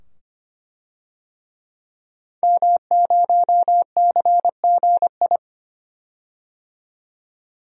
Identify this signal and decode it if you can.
Morse code, heard as a keyed tone, about 25 wpm, 700 Hz. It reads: M0CGI
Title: DXCC-100 Call Signs - 5 Characters - 25wpm